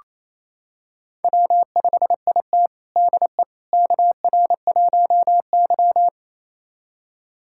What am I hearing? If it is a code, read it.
Morse code, heard as a keyed tone, about 28 wpm, 700 Hz. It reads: W5IT DE KR1Y
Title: USA Callsign DE Callsign - 4 Characters - 28wpm